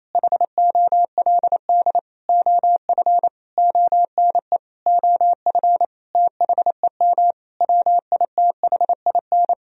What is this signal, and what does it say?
Morse code, heard as a keyed tone, about 28 wpm, 700 Hz. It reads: HOLD OF ONE OF THEM WITHIN
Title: Sentences from Top 1000 Words - Review Part 4 - 28wpm